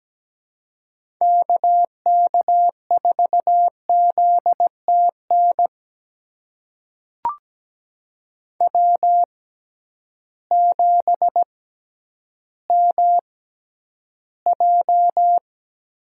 Morse code, heard as a keyed tone, about 17 wpm, spaced as for 10 wpm, 700 Hz. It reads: KK4ZTN W7MJ
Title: USA Call Signs - 6x Character Spacing - 17wpm